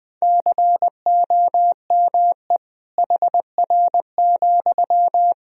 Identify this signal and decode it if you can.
Morse code, heard as a keyed tone, about 20 wpm, 700 Hz. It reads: COME HR,
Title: Sentences from Top 300 Words and Abbreviations 20wpm